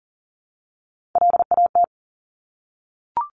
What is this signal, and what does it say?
Morse code, heard as a keyed tone, about 41 wpm, 700 Hz. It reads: LAT
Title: Top Three Letter Combinations in Words 40wpm